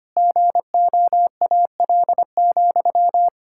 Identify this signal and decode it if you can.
Morse code, heard as a keyed tone, about 25 wpm, 700 Hz. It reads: GOAL,
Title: Sentences from Top 1000 Words - Review Part 6 - 25wpm